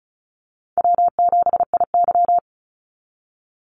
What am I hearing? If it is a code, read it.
Morse code, heard as a keyed tone, about 34 wpm, 700 Hz. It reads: W7IY
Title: Easy Call Signs Repeated 2x - 35wpm